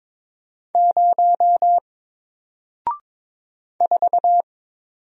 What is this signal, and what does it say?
Morse code, heard as a keyed tone, about 22 wpm, 700 Hz. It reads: 0 4
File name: Single-Letter-Number-22wpm